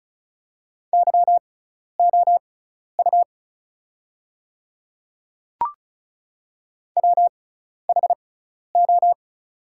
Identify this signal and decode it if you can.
Morse code, heard as a keyed tone, about 34 wpm, spaced as for 12 wpm, 690 Hz. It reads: YOU WHO
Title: 6x Character Spacing - Top 100 Words - 35wpm